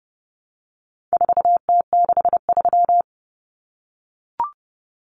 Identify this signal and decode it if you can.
Morse code, heard as a keyed tone, about 30 wpm, 700 Hz. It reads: 4T63
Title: Common Cut Numbers - 4-Digit 30wpm